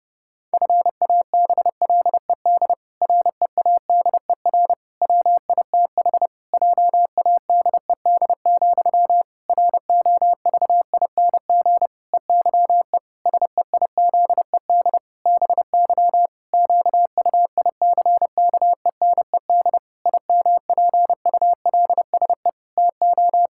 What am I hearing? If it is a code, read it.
Morse code, heard as a keyed tone, about 30 wpm, 700 Hz. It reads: FABLED READER WITH JADED, ROVING EYE SEIZED BY QUICKENED IMPULSE TO
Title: Pangram Challenge v2 - 30wpm